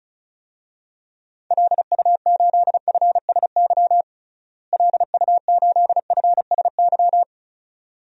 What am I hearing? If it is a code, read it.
Morse code, heard as a keyed tone, about 34 wpm, 700 Hz. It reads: LU8FSY LU8FSY
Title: DXCC-100 Call Signs - 6 Characters - Repeated 2x - 35wpm